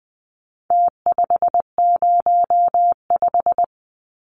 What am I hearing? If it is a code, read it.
Morse code, heard as a keyed tone, about 20 wpm, 700 Hz. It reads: T505
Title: Numbers - Contest Style - 4-Digit 20wpm